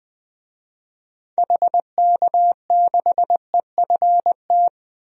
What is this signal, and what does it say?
Morse code, heard as a keyed tone, about 20 wpm, 700 Hz. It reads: HK6EFT
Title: DXCC-100 Call Signs - 6 Characters - 20wpm